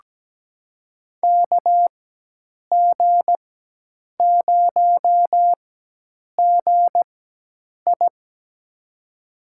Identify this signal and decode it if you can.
Morse code, heard as a keyed tone, about 17 wpm, spaced as for 8 wpm, 700 Hz. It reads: KG0GI
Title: USA Call Signs - 5 Characters - 4x Character Spacing - 17wpm